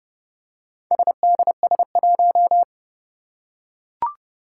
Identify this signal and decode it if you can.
Morse code, heard as a keyed tone, about 30 wpm, 700 Hz. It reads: SDS1
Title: Four Challenging Letter-Number Combinations 30wpm